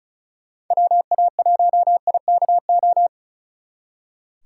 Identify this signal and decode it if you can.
Morse code, heard as a keyed tone, about 43 wpm, 700 Hz. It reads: WA1IKO
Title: Speed Racing - USA 2x3 Call Signs 35wpm